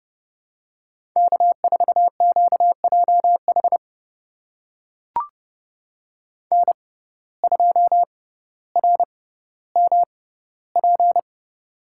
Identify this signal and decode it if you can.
Morse code, heard as a keyed tone, about 30 wpm, spaced as for 15 wpm, 700 Hz. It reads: K4QJH N2RMP
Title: USA Call Signs - 5 Characters - 6x Character Spacing - 30wpm